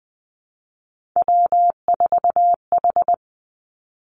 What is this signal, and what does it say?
Morse code, heard as a keyed tone, about 20 wpm, 700 Hz. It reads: W4H
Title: Three Challenging Letter-Number Combinations 20wpm